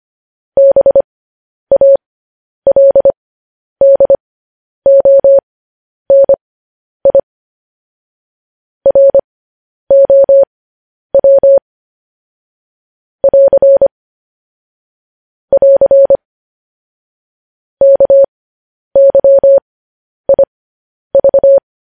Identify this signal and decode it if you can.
Morse code, heard as a keyed tone, about 25 wpm, spaced as for 10 wpm, 550 Hz. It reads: BALDONI ROW <AR> <AR> KYIV
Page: News Headlines in Morse Code at 10 WPM